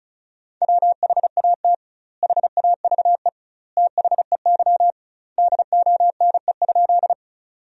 Morse code, heard as a keyed tone, about 34 wpm, 700 Hz. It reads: WHAT HAVE THEY DONE?
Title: 2x Word-Spacing - Sentences from Top 500 Words - 35wpm